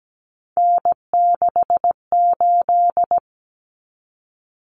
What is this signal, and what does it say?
Morse code, heard as a keyed tone, about 17 wpm, 690 Hz. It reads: N68
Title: Common Cut Numbers - 3-Digit 17wpm